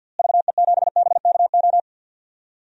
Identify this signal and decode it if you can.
Morse code, heard as a keyed tone, about 48 wpm, 700 Hz. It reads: VE7BKO
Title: DXCC-100 Call Signs - 6 Characters - Repeated 3x - 50wpm